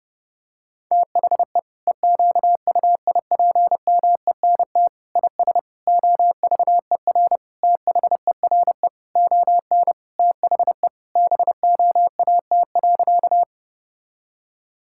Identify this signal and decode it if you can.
Morse code, heard as a keyed tone, about 30 wpm, 700 Hz. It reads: THE EQUIPMENT IS OVER THERE ON THE BOAT.
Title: Sentences from Top 1000 Words - Review Part 5 - 30wpm